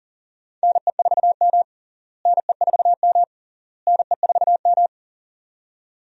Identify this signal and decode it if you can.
Morse code, heard as a keyed tone, about 40 wpm, 700 Hz. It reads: NE4M NE4M NE4M
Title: USA Call Signs - 4 Characters - Repeated 3x - 40wpm